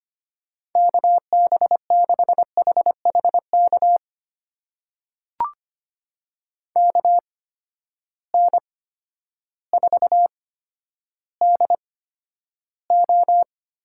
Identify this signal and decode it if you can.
Morse code, heard as a keyed tone, about 25 wpm, spaced as for 11 wpm, 700 Hz. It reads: KB6HHK KN4DO
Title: USA Call Signs - 8x Character Spacing - 25wpm